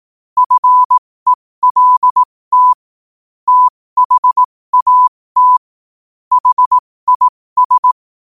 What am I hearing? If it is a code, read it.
Morse code, heard as a keyed tone, about 18 wpm, spaced as for 15 wpm, 1000 Hz. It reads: FELT THAT HIS
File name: qotdfiles / 15wpm